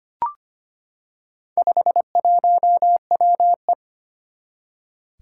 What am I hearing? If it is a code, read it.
Morse code, heard as a keyed tone, about 25 wpm, 700 Hz. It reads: 51WE
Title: Four Challenging Letter-Number Combinations 25wpm